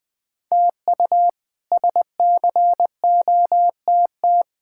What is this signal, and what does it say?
Morse code, heard as a keyed tone, about 20 wpm, 700 Hz. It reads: TU SCOTT
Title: K1USN SST QSO Practice - 20wpm